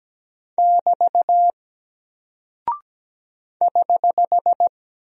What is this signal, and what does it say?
Morse code, heard as a keyed tone, about 17 wpm, 700 Hz. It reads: <BT> <HH>
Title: Single Character (Letter:Number:Punctuation:Prosign) 17wpm